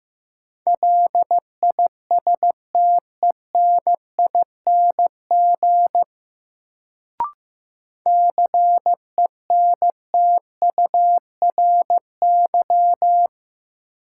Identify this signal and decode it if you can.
Morse code, heard as a keyed tone, about 15 wpm, 700 Hz. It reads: LISTENING CENTURY
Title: Sets of 5 Words - New Word Drill - 15wpm